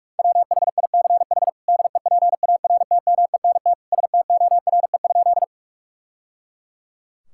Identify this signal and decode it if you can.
Morse code, heard as a keyed tone, about 44 wpm, 700 Hz. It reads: WHICH DEPARTMENT STORE?
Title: Sentences from Top 1000 Words - Review Part 5 - 45wpm